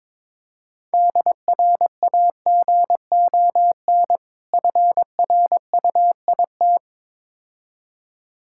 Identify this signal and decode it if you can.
Morse code, heard as a keyed tone, about 22 wpm, 690 Hz. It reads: DRAGON FRUIT
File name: Difficult Words with Hints - 22wpm